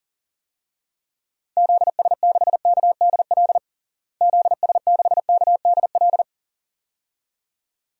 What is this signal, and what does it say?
Morse code, heard as a keyed tone, about 40 wpm, 700 Hz. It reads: ZS6KDL ZS6KDL
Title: DXCC-100 Call Signs - 6 Characters - Repeated 2x - 40wpm